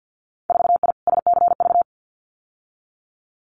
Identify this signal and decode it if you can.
Morse code, heard as a keyed tone, about 50 wpm, 700 Hz. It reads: 4I SCV
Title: Winter Field Day Exchanges Encore 50wpm